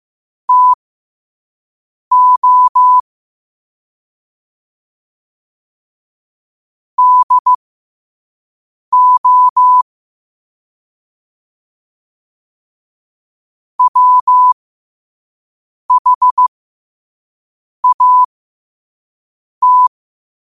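Morse code, heard as a keyed tone, about 15 wpm, spaced as for 5 wpm, 1000 Hz. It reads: TO DO WHAT
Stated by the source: Morse code sent at 5 wpm for learning.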